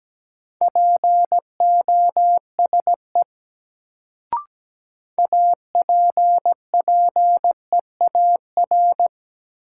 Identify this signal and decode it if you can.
Morse code, heard as a keyed tone, about 17 wpm, 700 Hz. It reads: POSE APPEAR
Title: Sentences from Top 1000 Words - Review Part 2 - New Word Drill - 17wpm